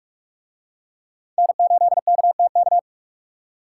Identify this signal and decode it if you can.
Morse code, heard as a keyed tone, about 43 wpm, 700 Hz. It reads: N8KTK